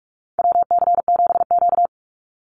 Morse code, heard as a keyed tone, about 46 wpm, 700 Hz. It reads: WC7Q
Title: Call Signs - 45wpm